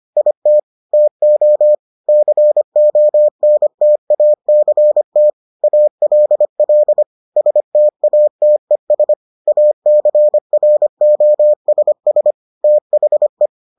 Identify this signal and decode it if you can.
Morse code, heard as a keyed tone, about 25 wpm, 600 Hz. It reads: IT TO CONTACT ALL STATES ACROSS THE